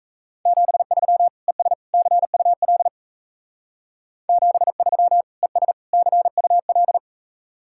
Morse code, heard as a keyed tone, about 36 wpm, 700 Hz. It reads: 73 ES CUL 73 ES CUL